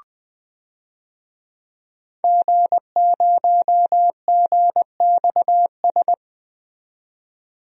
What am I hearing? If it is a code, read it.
Morse code, heard as a keyed tone, about 20 wpm, 700 Hz. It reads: G0GXS
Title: DXCC-100 Call Signs - 5 Characters - 20wpm